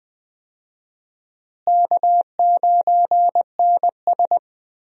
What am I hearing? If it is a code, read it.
Morse code, heard as a keyed tone, about 20 wpm, 700 Hz. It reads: K9NS
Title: USA Call Signs - 4 Characters - Repeated 3x - 20wpm